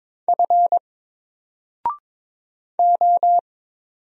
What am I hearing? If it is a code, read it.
Morse code, heard as a keyed tone, about 22 wpm, 700 Hz. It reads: F O
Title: Single Character (Letter:Number:Punctuation:Prosign) 22wpm